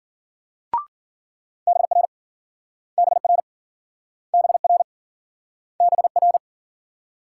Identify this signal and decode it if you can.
Morse code, heard as a keyed tone, about 49 wpm, 700 Hz. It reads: BR BR BR BR
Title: Speed-Racing Multi-letter Phonemes - 40wpm